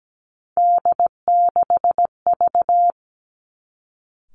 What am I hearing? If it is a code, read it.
Morse code, heard as a keyed tone, about 17 wpm, 700 Hz. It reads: D6V
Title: Three Challenging Letter-Number Combinations 17wpm